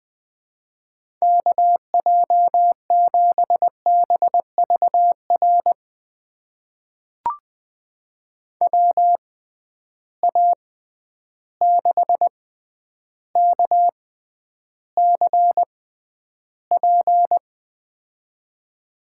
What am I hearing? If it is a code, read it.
Morse code, heard as a keyed tone, about 20 wpm, spaced as for 10 wpm, 700 Hz. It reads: KJ7BVR WA6KCP
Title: USA Call Signs - 6 Characters - 6x Character Spacing - 20wpm